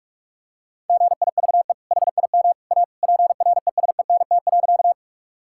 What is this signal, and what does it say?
Morse code, heard as a keyed tone, about 45 wpm, 700 Hz. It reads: GIVE HIM A PRESENT.
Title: Sentences from Top 1000 Words - Review Part 5 - 45wpm